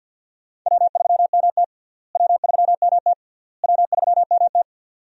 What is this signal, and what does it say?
Morse code, heard as a keyed tone, about 46 wpm, 700 Hz. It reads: W3MT W3MT W3MT
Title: USA Call Signs - 4 Characters - Repeated 3x - 50wpm